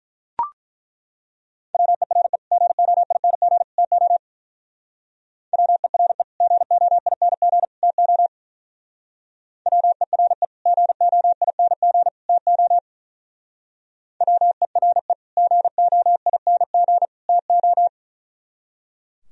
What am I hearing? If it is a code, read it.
Morse code, heard as a keyed tone, about 42 wpm, 700 Hz. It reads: WERE GOING TO WERE GOING TO WERE GOING TO WERE GOING TO
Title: Speed-Racing - Sets of 3 Words 35wpm